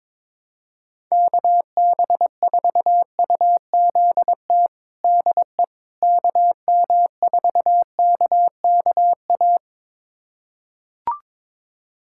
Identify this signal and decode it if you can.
Morse code, heard as a keyed tone, about 22 wpm, 700 Hz. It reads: KB4UZT DE KM4KKA
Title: USA Callsign DE Callsign - 6 Characters - 22wpm